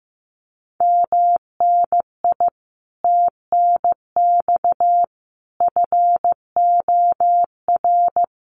Morse code, heard as a keyed tone, about 15 wpm, 700 Hz. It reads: MNI TNX FOR
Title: Common QSO Phrases 15wpm